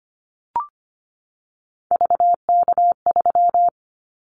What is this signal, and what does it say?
Morse code, heard as a keyed tone, about 25 wpm, 700 Hz. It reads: VK3
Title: Callsign Prefixes for Top 100 DXCC - 25wpm